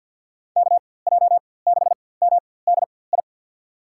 Morse code, heard as a keyed tone, about 51 wpm, spaced as for 23 wpm, 700 Hz. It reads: KJ6MDI
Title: USA Call Signs - 6 Characters - 4x Character Spacing - 50wpm